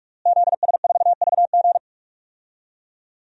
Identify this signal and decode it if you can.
Morse code, heard as a keyed tone, about 46 wpm, 700 Hz. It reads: ZS4VG
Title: DXCC-100 Call Signs - 5 Characters - Repeated 3x - 45wpm